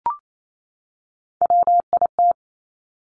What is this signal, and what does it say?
Morse code, heard as a keyed tone, about 28 wpm, 700 Hz. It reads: WIT
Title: Top Three Letter Combinations in Words 28wpm